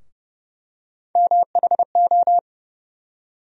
Morse code, heard as a keyed tone, about 30 wpm, 700 Hz. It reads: MHO
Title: USA Call Sign Postfixes - 3 Letters 30wpm